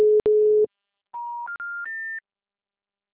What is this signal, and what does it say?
Special information tone.
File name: out- -2025- - - .31148